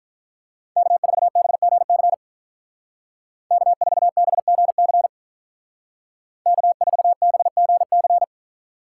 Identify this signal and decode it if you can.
Morse code, heard as a keyed tone, about 47 wpm, 700 Hz. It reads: K4BGC K4BGC K4BGC